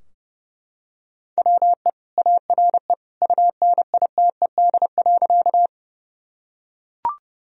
Morse code, heard as a keyed tone, about 32 wpm, 700 Hz. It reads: WE ARE UNITED.
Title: Sentences from Top 1000 Words - Review Part 4 - 30wpm